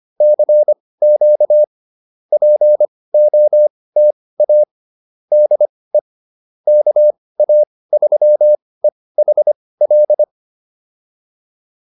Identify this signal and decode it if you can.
Morse code, heard as a keyed tone, about 25 wpm, spaced as for 18 wpm, 600 Hz. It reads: CQ POTA DE KA3EHL
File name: POTA109 18